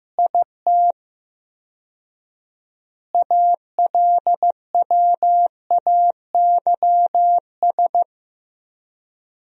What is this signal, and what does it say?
Morse code, heard as a keyed tone, about 15 wpm, 700 Hz. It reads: IT ALWAYS
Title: 4x Word-Spacing - Sentences from Top 300 Words - 15wpm